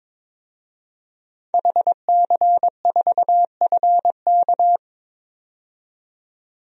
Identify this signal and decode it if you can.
Morse code, heard as a keyed tone, about 22 wpm, 700 Hz. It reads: HC4FK
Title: DXCC-100 Call Signs - 5 Characters - 22wpm